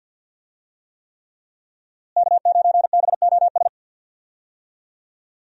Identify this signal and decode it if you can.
Morse code, heard as a keyed tone, about 46 wpm, 700 Hz. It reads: K9BOS
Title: USA Call Signs - 5 Characters - Repeated 3x - 50wpm